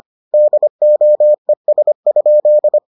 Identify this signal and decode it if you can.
Morse code, heard as a keyed tone, about 25 wpm, 600 Hz. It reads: DOES?